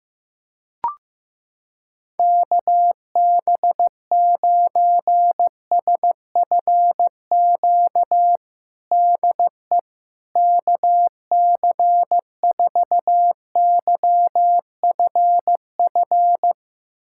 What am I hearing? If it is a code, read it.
Morse code, heard as a keyed tone, about 15 wpm, 700 Hz. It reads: KB9SFQ DE KC4YFF
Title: USA Callsign DE Callsign - 6 Characters - 15wpm